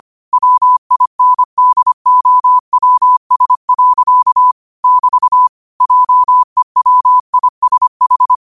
Morse code, heard as a keyed tone, about 25 wpm, 1000 Hz. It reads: WINDOWS. <BT> JEWISH